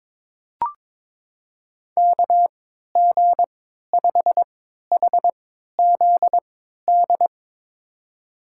Morse code, heard as a keyed tone, about 22 wpm, spaced as for 13 wpm, 700 Hz. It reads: KG5HZD
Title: USA Call Signs - 6 Characters - 3x Character Spacing - 22wpm